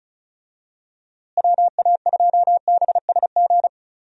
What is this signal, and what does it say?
Morse code, heard as a keyed tone, about 36 wpm, 700 Hz. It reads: WA2BSG
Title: USA Call Signs - 6 Characters - Repeated 2x - 35wpm